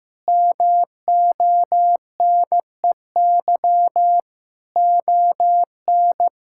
Morse code, heard as a keyed tone, about 15 wpm, 700 Hz. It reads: MONEY ON
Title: Sentences from Top 1600 Words 15wpm